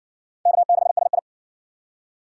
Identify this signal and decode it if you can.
Morse code, heard as a keyed tone, about 59 wpm, 700 Hz. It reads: K6SI